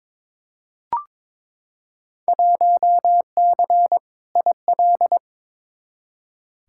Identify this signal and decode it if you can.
Morse code, heard as a keyed tone, about 22 wpm, 700 Hz. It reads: 1C IL
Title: Field Day Exchanges - 22wpm